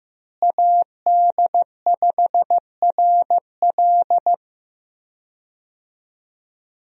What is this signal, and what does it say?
Morse code, heard as a keyed tone, about 15 wpm, 700 Hz. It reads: AD5RL
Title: USA Call Signs - 5 Characters - Repeated 3x - 15wpm